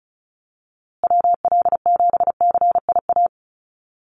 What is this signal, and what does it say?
Morse code, heard as a keyed tone, about 34 wpm, 700 Hz. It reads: WL7CIA
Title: USA Call Signs Repeated 2x 35wpm